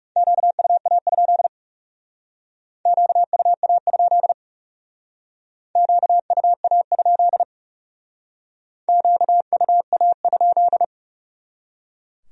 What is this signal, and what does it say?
Morse code, heard as a keyed tone, about 38 wpm, 700 Hz. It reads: QUA? QUA? QUA? QUA?